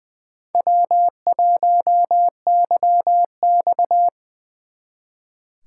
Morse code, heard as a keyed tone, about 24 wpm, 700 Hz. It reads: W1YX
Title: Speed Racing - USA 1x2 Call Signs 20wpm